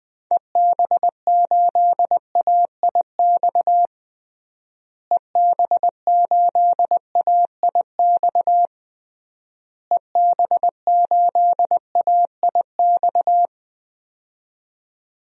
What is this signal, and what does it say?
Morse code, heard as a keyed tone, about 20 wpm, 700 Hz. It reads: EB8AIX EB8AIX EB8AIX
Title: DXCC-100 Call Signs - 6 Characters - Repeated 3x - 20wpm